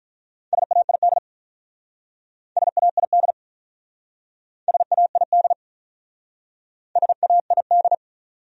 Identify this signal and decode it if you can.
Morse code, heard as a keyed tone, about 47 wpm, 700 Hz. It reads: SAID SAID SAID SAID